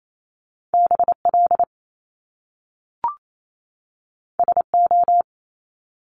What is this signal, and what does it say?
Morse code, heard as a keyed tone, about 28 wpm, 700 Hz. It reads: BL SO
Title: Top Two Letter Combinations in Words 28wpm